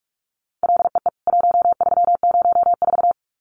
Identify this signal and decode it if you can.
Morse code, heard as a keyed tone, about 44 wpm, 700 Hz. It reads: LEE 1304
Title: CWT Exchange 45wpm